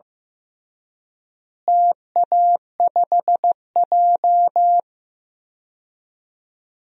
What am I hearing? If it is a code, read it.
Morse code, heard as a keyed tone, about 14 wpm, 700 Hz. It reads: TA5J
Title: DXCC-100 Call Signs - 4 Characters - Repeated 3x - 15wpm